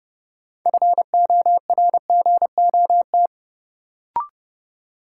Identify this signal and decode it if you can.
Morse code, heard as a keyed tone, about 30 wpm, 700 Hz. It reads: FORGOT
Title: Sentences from Top 1400 Words - New Word Drill - 30wpm